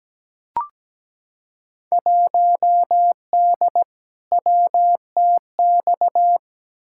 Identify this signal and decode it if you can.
Morse code, heard as a keyed tone, about 17 wpm, 700 Hz. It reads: 1D WTX
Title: Field Day Exchanges - 17wpm